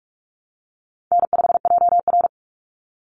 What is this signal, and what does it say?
Morse code, heard as a keyed tone, about 44 wpm, 700 Hz. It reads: N5JR
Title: Call Signs Repeated 2x - 45wpm